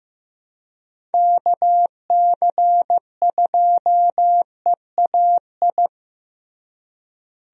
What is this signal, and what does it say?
Morse code, heard as a keyed tone, about 15 wpm, 700 Hz. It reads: KC2EAI
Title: USA Call Signs - 6 Characters - Repeated 2x - 15wpm